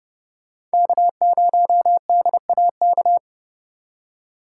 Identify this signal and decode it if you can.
Morse code, heard as a keyed tone, about 30 wpm, 700 Hz. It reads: K0DAK